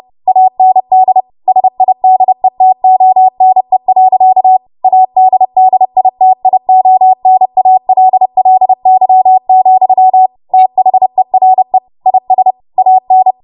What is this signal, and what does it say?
Morse code, heard as a keyed tone, about 30 wpm, 750 Hz. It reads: AND SIDETONE. ADDITIONALLY, THERE IS AN
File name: Morse / 30WPM